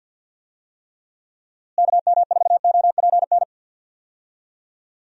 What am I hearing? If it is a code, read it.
Morse code, heard as a keyed tone, about 51 wpm, 700 Hz. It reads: KM4OPN
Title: USA Call Signs - 6 Characters - Repeated 2x - 50wpm